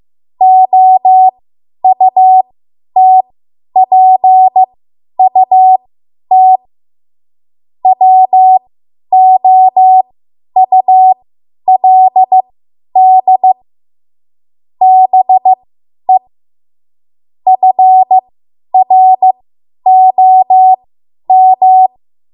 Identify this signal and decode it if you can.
Morse code, heard as a keyed tone, about 15 wpm, spaced as for 10 wpm, 750 Hz. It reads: OUTPUT WOULD BE FROM
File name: Archive / 10 WPM / 10WPM